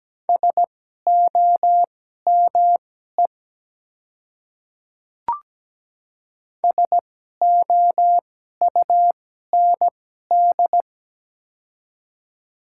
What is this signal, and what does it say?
Morse code, heard as a keyed tone, about 17 wpm, spaced as for 12 wpm, 700 Hz. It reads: SOME SOUND
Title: 2x Character Spacing - Top 100 Words - 17wpm